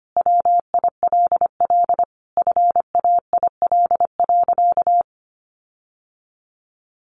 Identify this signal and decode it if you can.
Morse code, heard as a keyed tone, about 25 wpm, 700 Hz. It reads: WILL FAIL.
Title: Sentences from Top 1500 Words 25wpm